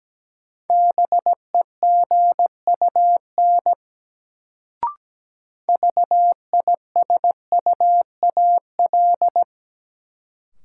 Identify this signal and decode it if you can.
Morse code, heard as a keyed tone, about 17 wpm, 700 Hz. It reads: BEGUN VISUAL
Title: Sentences from Top 1000 Words - Review Part 6 - New Word Drill - 17wpm